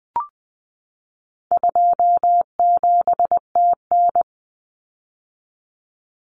Morse code, heard as a keyed tone, about 20 wpm, 700 Hz. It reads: 27TN
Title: Common Cut Numbers - 4-Digit 20wpm